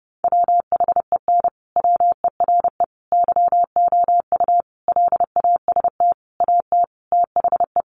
Morse code, heard as a keyed tone, about 30 wpm, 700 Hz. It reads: WHEN WERE YOU LAST AT THE
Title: Sentences from Top 1300 Words 30wpm